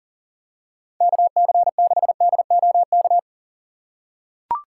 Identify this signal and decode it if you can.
Morse code, heard as a keyed tone, about 40 wpm, 700 Hz. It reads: KC6DOK
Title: USA Call Signs - 6 Characters - 3x Character Spacing - 40wpm